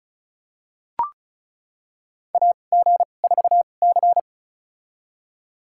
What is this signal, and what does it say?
Morse code, heard as a keyed tone, about 36 wpm, spaced as for 25 wpm, 700 Hz. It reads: AG4C
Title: USA Call Signs - 2x Character Spacing - 35wpm